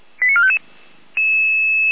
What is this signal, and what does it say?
Five-tone selective call (ZVEI), heard as quick runs of five tones, tones of about 70 ms. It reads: 89455